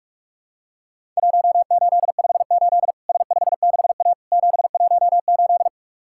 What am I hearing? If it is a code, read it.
Morse code, heard as a keyed tone, about 46 wpm, 700 Hz. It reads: 1858 S56A 718
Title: Worked All Europe DX Contest - QTC Exchange - 45wpm